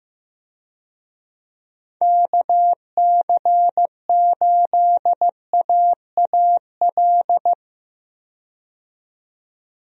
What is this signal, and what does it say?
Morse code, heard as a keyed tone, about 15 wpm, 700 Hz. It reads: KC8AAL
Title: USA Call Signs - 6 Characters - 15wpm